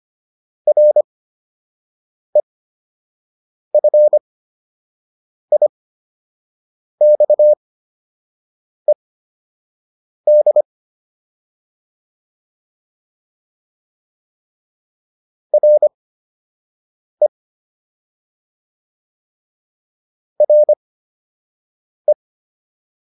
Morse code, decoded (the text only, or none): REFIXED RE RE